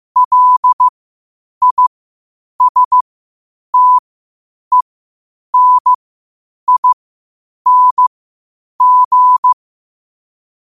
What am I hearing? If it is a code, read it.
Morse code, heard as a keyed tone, about 15 wpm, spaced as for 8 wpm, 1000 Hz. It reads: LISTENING
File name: qotdfiles / 8wpm